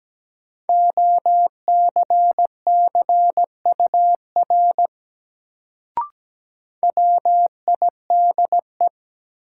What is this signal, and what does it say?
Morse code, heard as a keyed tone, about 17 wpm, 700 Hz. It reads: OCCUR WIDE
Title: Sentences from Top 900 Words - New Word Drill - 17wpm